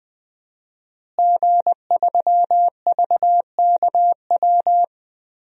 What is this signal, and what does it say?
Morse code, heard as a keyed tone, about 20 wpm, 700 Hz. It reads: G3VKW
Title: DXCC-100 Call Signs - 5 Characters - Repeated 2x - 20wpm